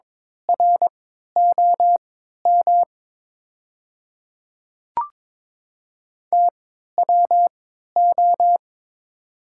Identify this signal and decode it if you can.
Morse code, heard as a keyed tone, about 22 wpm, spaced as for 12 wpm, 700 Hz. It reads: ROM TWO
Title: 3x Character Spacing - Top 100 Words - 22wpm